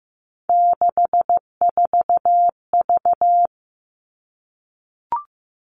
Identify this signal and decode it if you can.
Morse code, heard as a keyed tone, about 15 wpm, 700 Hz. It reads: 64V
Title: All Cut Numbers - 3-Digit 15wpm